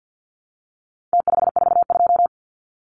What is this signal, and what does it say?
Morse code, heard as a keyed tone, about 50 wpm, 700 Hz. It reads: T542
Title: Numbers - Contest Style - 4-Digit 50wpm